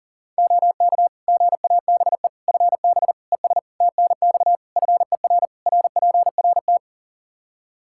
Morse code, heard as a keyed tone, about 40 wpm, 700 Hz. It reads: OK GABE FB ES TNX FER RPRT
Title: Practice QSO - Exchange 2 - Info - 40wpm